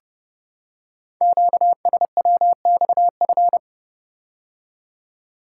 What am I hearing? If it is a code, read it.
Morse code, heard as a keyed tone, about 28 wpm, 700 Hz. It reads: QSWXF
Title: Code Groups - 5 Letters - 30wpm